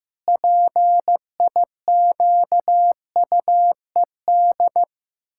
Morse code, heard as a keyed tone, about 16 wpm, 700 Hz. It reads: PIQUED
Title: Pangram Challenge v2 - 15wpm